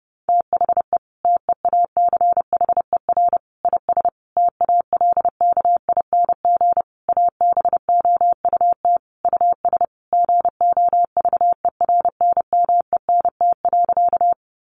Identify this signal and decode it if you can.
Morse code, heard as a keyed tone, about 30 wpm, 700 Hz. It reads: THE TEACHER IS TALKING ABOUT US GOVERNMENT.
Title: Easy Phrases 30wpm